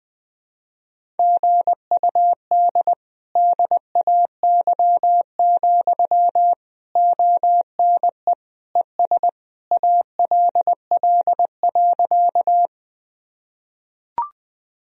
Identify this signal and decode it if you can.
Morse code, heard as a keyed tone, about 20 wpm, 700 Hz. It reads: GUD DAY, ONE ES ALL.
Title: Sentences from Top 200 Words and Abbreviations 20wpm